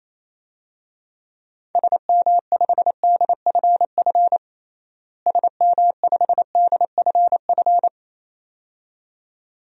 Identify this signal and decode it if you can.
Morse code, heard as a keyed tone, about 28 wpm, 700 Hz. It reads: SM5DFF SM5DFF
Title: DXCC-100 Call Signs - 6 Characters - Repeated 2x - 28wpm